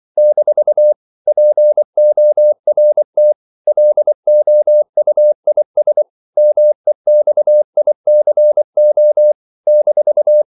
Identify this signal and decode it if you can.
Morse code, heard as a keyed tone, about 24 wpm, 600 Hz. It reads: - PORT LOUIS MEXICO -
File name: CW / Practice / NationalCapitals24